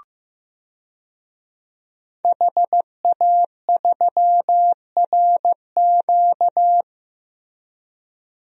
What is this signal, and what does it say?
Morse code, heard as a keyed tone, about 15 wpm, 700 Hz. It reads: HA3RQ
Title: DXCC-100 Call Signs - 5 Characters - Repeated 2x - 15wpm